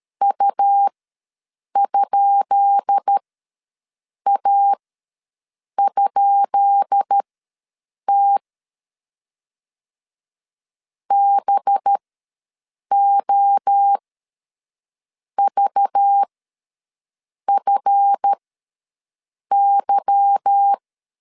Morse code, heard as a keyed tone, about 13 wpm, spaced as for 6 wpm, 800 Hz. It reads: U?A?T BOVFY
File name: LEZIONI / 10 SIGNS / 10B INT ws 12wpm 3527